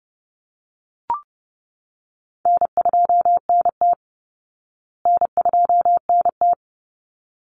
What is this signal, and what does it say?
Morse code, heard as a keyed tone, about 30 wpm, 700 Hz. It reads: N2NT N2NT
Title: Easy Call Signs Repeated 2x - 30wpm